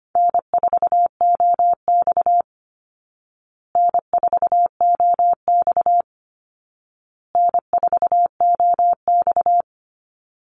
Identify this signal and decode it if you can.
Morse code, heard as a keyed tone, about 25 wpm, 700 Hz. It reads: N4OX N4OX N4OX
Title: Easy Call Signs Repeated 3x - 25wpm